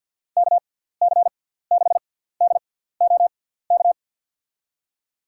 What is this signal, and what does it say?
Morse code, heard as a keyed tone, about 51 wpm, spaced as for 17 wpm, 700 Hz. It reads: KC6DOK
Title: USA Call Signs - 6 Characters - 6x Character Spacing - 50wpm